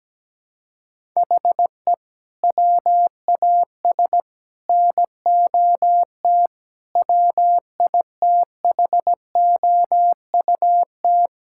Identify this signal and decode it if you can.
Morse code, heard as a keyed tone, about 17 wpm, 700 Hz. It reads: HE WAS NOT WITHOUT
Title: Sentences from Top 1000 Words - Review Part 6 - 17wpm